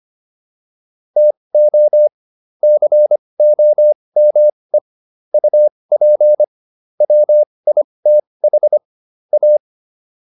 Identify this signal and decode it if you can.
Morse code, heard as a keyed tone, about 25 wpm, spaced as for 20 wpm, 600 Hz. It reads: TO COME UP WITH A
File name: WD405 20